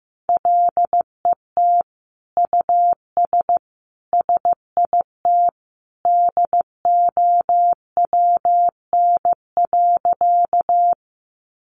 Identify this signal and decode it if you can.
Morse code, heard as a keyed tone, about 15 wpm, 700 Hz. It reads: LET US SIT DOWN.
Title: Sentences from Top 500 Words 15wpm